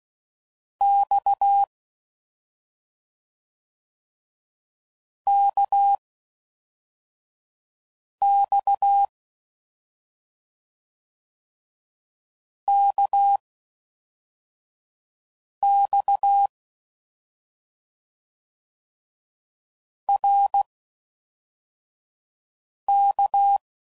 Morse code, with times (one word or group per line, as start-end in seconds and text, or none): 0.8-1.6 X
5.3-6.0 K
8.2-9.1 X
12.7-13.4 K
15.6-16.5 X
20.1-20.6 R
22.9-23.6 K